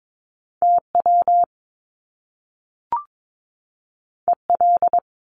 Morse code, heard as a keyed tone, about 22 wpm, 700 Hz. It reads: TW EL
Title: Top Two Letter Combinations in Words 22wpm